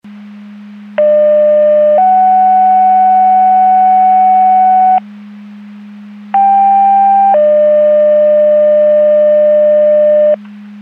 Two-tone page, heard as a one-second tone, then a three-second tone, about 600 then 750 Hz, about 790 then 600 Hz. It is heard from 1.0 to 10.3 s.
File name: Portage County 154.3850 Fire EMS Paging